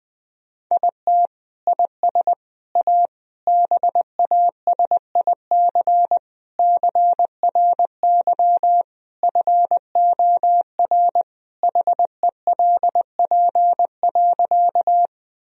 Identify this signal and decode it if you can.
Morse code, heard as a keyed tone, about 20 wpm, 700 Hz. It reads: IT IS A BASIC CRY FOR HELP.
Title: Sentences from Top 1000 Words - Review Part 3 - 20wpm